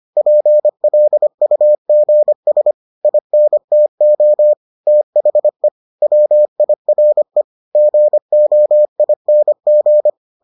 Morse code, heard as a keyed tone, about 25 wpm, 600 Hz. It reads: PLUGS INTO THE WIRE GOING